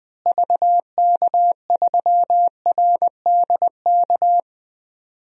Morse code, heard as a keyed tone, about 20 wpm, 700 Hz. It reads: VK3RDK
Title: DXCC-100 Call Signs - 6 Characters - Repeated 2x - 20wpm